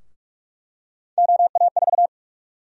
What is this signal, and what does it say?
Morse code, heard as a keyed tone, about 46 wpm, 700 Hz. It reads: OA4
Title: USA Call Sign Prefixes - 45wpm